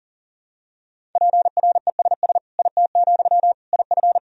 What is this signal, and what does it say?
Morse code, heard as a keyed tone, about 40 wpm, 700 Hz. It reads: PRESS IT, IF
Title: Sentences from Top 1000 Words - Review Part 1 - 40wpm